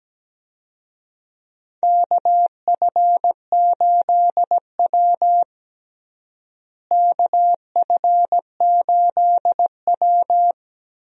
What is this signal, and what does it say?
Morse code, heard as a keyed tone, about 17 wpm, 700 Hz. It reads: KF8W KF8W
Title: USA Call Signs - 4 Characters - Repeated 3x - 17wpm